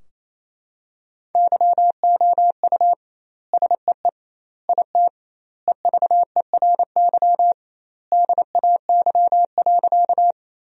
Morse code, heard as a keyed tone, about 27 wpm, 700 Hz. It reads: YOU SEE IT EVERY DAY.
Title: 2x Word-Spacing - Sentences from Top 200 Words - 28wpm